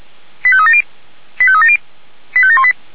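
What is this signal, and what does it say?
Five-tone selective call (ZVEI), heard as quick runs of five tones, tones of about 70 ms. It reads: 85370 85370 85718
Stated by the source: ZVEI five-tone call 85751, received on SDR.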